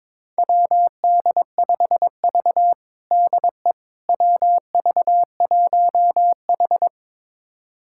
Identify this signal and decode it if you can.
Morse code, heard as a keyed tone, about 22 wpm, 700 Hz. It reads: WD5V DE WV1H
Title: USA Callsign DE Callsign - 4 Characters - 22wpm